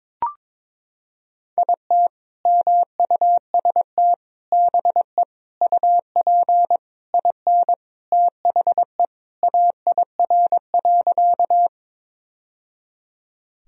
Morse code, heard as a keyed tone, about 21 wpm, 700 Hz. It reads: IT MUST BE UP IN THE AIR.
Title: Sentences from Top 1000 Words - Review Part 1 - 22wpm